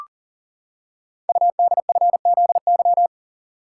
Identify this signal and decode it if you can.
Morse code, heard as a keyed tone, about 42 wpm, 700 Hz. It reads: UDFZY
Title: Code Groups - 5 Letters - 40wpm